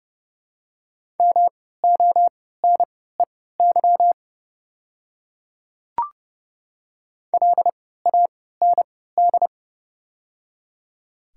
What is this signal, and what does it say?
Morse code, heard as a keyed tone, about 30 wpm, spaced as for 17 wpm, 700 Hz. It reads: MONEY LAND
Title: 3x Character Spacing - Top 500 Words - 30wpm